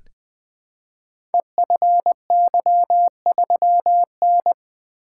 Morse code, heard as a keyed tone, about 20 wpm, 700 Hz. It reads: EFY3N
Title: Code Groups - 5 Letter:Numbers - 20wpm